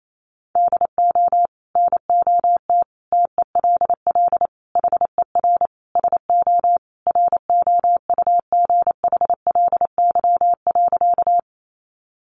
Morse code, heard as a keyed tone, about 28 wpm, 700 Hz. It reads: DO NOT TELL HER SO ROUGHLY.
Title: Sentences from Top 1500 Words 28wpm